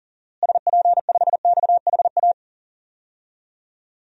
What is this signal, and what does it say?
Morse code, heard as a keyed tone, about 40 wpm, 700 Hz. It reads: SP5XHA
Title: DXCC-100 Call Signs - 6 Characters - Repeated 2x - 40wpm